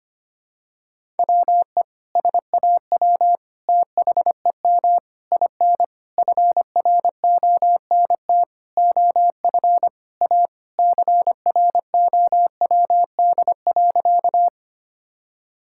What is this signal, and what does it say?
Morse code, heard as a keyed tone, about 25 wpm, 700 Hz. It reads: WE SAW THEM IN FRONT OF A CROWD.
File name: practice / 25wpm / Sentences with 4-Word Phrases - 25wpm